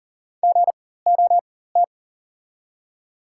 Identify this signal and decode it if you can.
Morse code, heard as a keyed tone, about 40 wpm, spaced as for 19 wpm, 690 Hz. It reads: GOT
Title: 4x Character Spacing - Top 300 Words - 40wpm